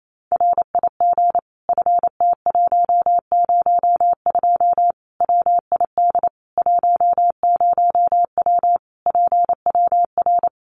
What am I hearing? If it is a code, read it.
Morse code, heard as a keyed tone, about 28 wpm, 700 Hz. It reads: RIG FT102 WID 10W PWR
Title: Common QSO Phrases 28wpm